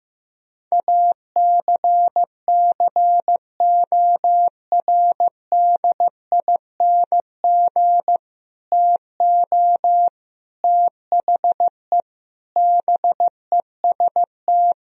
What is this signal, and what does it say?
Morse code, heard as a keyed tone, about 15 wpm, 700 Hz. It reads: ACCORDING TO THE BEST
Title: Sentences from Top 1000 Words - Review Part 4 - 15wpm